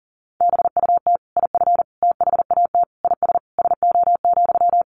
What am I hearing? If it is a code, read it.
Morse code, heard as a keyed tone, about 40 wpm, 700 Hz. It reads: BUT IF THAT IS SO,
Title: Sentences from Top 200 Words 40wpm